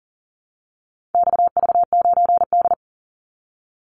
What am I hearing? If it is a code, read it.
Morse code, heard as a keyed tone, about 39 wpm, 700 Hz. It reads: XV9D
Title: Call Signs - 40wpm